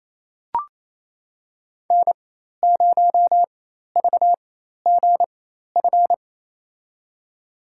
Morse code, heard as a keyed tone, about 28 wpm, spaced as for 13 wpm, 700 Hz. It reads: N0VGF
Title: USA Call Signs - 5 Characters - 4x Character Spacing - 28wpm